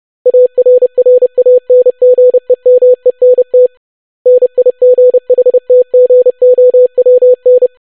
Morse code, heard as a keyed tone, about 30 wpm, 500 Hz. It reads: ARRANGEMENT NIGHTGOWN
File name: longwords-25wpm